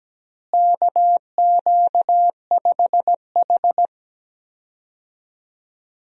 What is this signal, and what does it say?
Morse code, heard as a keyed tone, about 17 wpm, 700 Hz. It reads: KQ5H
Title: USA Call Signs - 4 Characters - Repeated 3x - 17wpm